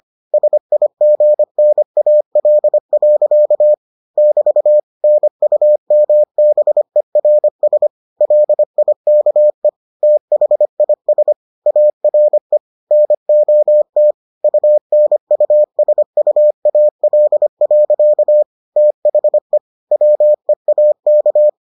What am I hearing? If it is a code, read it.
Morse code, heard as a keyed tone, about 25 wpm, 600 Hz. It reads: SIGNAL. <BT> NUMBERS LIKE THIS ARE NOT UNUSUAL. THE WEAK